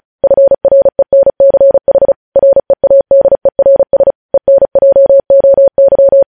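Morse code, heard as a keyed tone, about 36 wpm, 550 Hz. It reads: FRENCH READERS ENJOY
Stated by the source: Morse code news headlines sent at 35 wpm.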